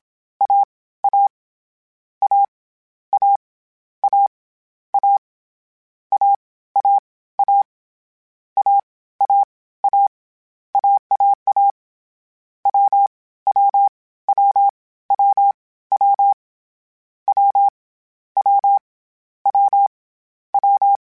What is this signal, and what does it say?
Morse code, heard as a keyed tone, about 27 wpm, 800 Hz. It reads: A A A A A A A A A A A A AAA W W W W W W W W W
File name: Difficult Combos / Difficult Combos-mp3s / AWJ1 @25